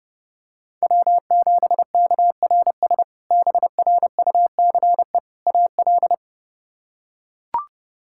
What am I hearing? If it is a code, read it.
Morse code, heard as a keyed tone, about 30 wpm, 700 Hz. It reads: W7KRS BRUCE AL
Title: K1USN SST QSO Practice - 30wpm